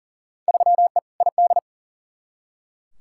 Morse code, heard as a keyed tone, about 40 wpm, 700 Hz. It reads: 3E ID
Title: Field Day Exchanges Encore - 40wpm